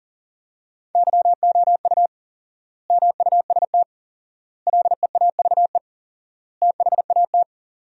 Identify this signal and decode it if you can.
Morse code, heard as a keyed tone, about 40 wpm, 700 Hz. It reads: YOU MUST LEAVE THAT
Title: 4x Word-Spacing - Sentences from Top 400 Words - 40wpm